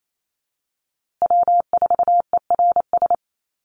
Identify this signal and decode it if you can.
Morse code, heard as a keyed tone, about 28 wpm, 700 Hz. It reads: W4ERS
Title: USA Call Signs Repeated 2x 28wpm